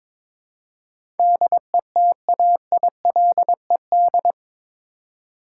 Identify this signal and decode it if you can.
Morse code, heard as a keyed tone, about 21 wpm, 700 Hz. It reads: DETAILED
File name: practice / 22wpm / Eight Letter Words - 22wpm